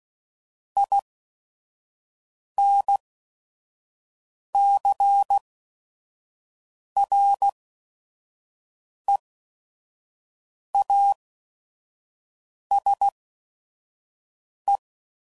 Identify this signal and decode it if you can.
Morse code, heard as a keyed tone, about 16 wpm, spaced as for 5 wpm, 800 Hz. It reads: INCREASE
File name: CW Code Groups @5WPM / Technical Subjects / XL